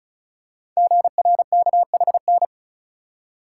Morse code, heard as a keyed tone, about 36 wpm, 690 Hz. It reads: GRKHN
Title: Code Groups - 5 Letter:Numbers - 35wpm